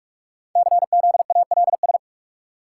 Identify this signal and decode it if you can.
Morse code, heard as a keyed tone, about 44 wpm, 700 Hz. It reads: CZALS
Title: Code Groups - 5 Letters - 45wpm